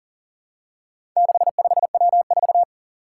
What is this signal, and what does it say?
Morse code, heard as a keyed tone, about 40 wpm, 700 Hz. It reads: 65W4
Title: Four Challenging Letter-Number Combinations 40wpm